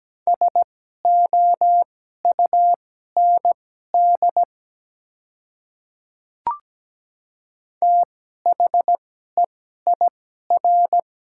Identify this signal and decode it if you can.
Morse code, heard as a keyed tone, about 17 wpm, spaced as for 12 wpm, 700 Hz. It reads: SOUND THEIR
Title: 2x Character Spacing - Top 300 Words - 17wpm